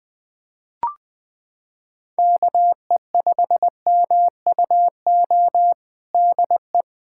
Morse code, heard as a keyed tone, about 20 wpm, 700 Hz. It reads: KE5MUO DE
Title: K1USN SST QSO Practice 20wpm